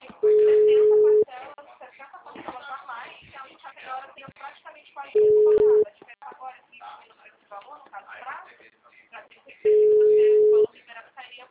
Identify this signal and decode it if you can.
Ringback tone (Europe, 425 Hz), with speech.
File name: out- -2007- - - .87900